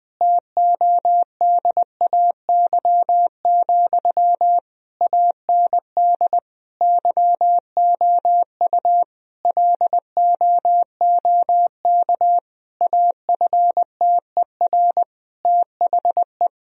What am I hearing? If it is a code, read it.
Morse code, heard as a keyed tone, about 20 wpm, 700 Hz. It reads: TODAY, AND YOU LOOK AFTER THE
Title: Sentences from Top 1000 Words - Review Part 4 - 20wpm